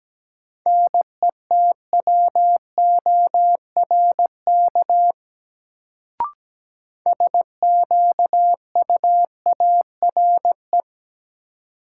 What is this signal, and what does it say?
Morse code, heard as a keyed tone, about 17 wpm, 700 Hz. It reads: NETWORK SQUARE
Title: Sentences from Top 1000 Words - Review Part 7 - New Word Drill - 17wpm